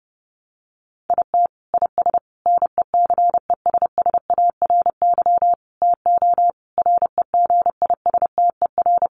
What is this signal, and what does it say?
Morse code, heard as a keyed tone, about 30 wpm, 700 Hz. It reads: IT IS NECESSARY TO REGISTER
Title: Sentences from Top 1200 Words 30wpm